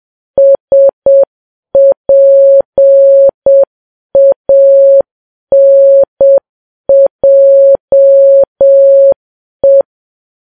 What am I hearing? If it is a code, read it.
Morse code, heard as a keyed tone, about 7 wpm, 550 Hz. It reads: SPANJE